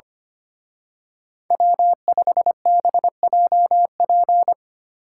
Morse code, heard as a keyed tone, about 29 wpm, 700 Hz. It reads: W5BJP
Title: Speed Racing - USA 1x3 Call Signs 25wpm